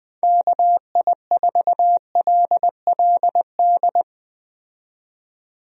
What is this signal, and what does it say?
Morse code, heard as a keyed tone, about 21 wpm, 700 Hz. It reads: KI4LLD
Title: USA Call Signs - 6 Characters - 20wpm